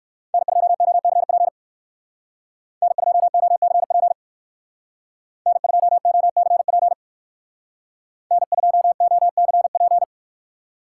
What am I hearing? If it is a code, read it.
Morse code, heard as a keyed tone, about 59 wpm, 700 Hz. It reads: N2OCP N2OCP N2OCP N2OCP